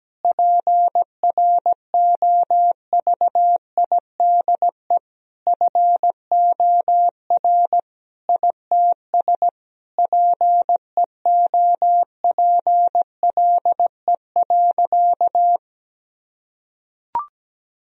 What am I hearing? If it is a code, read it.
Morse code, heard as a keyed tone, about 17 wpm, 700 Hz. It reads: PROVIDE FOR ITS PEOPLE.
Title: Sentences from Top 1000 Words - Review Part 3 - 17wpm